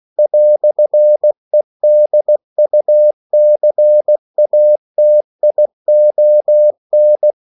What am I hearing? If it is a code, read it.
Morse code, heard as a keyed tone, about 16 wpm, 600 Hz. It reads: "EDUCATION
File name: CW / Practice / EnglishQuotations0216